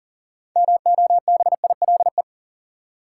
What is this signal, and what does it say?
Morse code, heard as a keyed tone, about 40 wpm, 700 Hz. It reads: MOBILE
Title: Six Letter Words - 40wpm